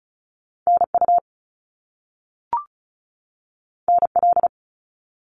Morse code, heard as a keyed tone, about 36 wpm, 700 Hz. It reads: NU NL
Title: Canadian Provinces and Territories 35wpm